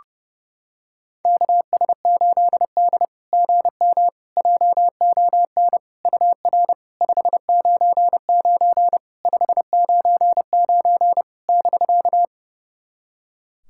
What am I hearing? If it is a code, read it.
Morse code, heard as a keyed tone, about 31 wpm, 690 Hz. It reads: KS8D GM JON UR 599 599 <BK>
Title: SOTA QSO Practice 30wpm